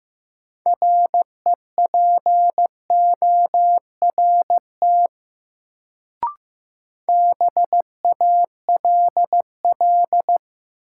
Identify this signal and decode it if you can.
Morse code, heard as a keyed tone, about 15 wpm, 700 Hz. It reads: REPORT BALL
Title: Sentences from Top 900 Words - New Word Drill - 15wpm